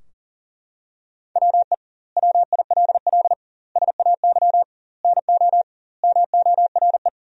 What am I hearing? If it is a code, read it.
Morse code, heard as a keyed tone, about 40 wpm, 700 Hz. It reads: WE WILL SAY NO MORE
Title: 2x Word-Spacing - Sentences from Top 200 Words - 40wpm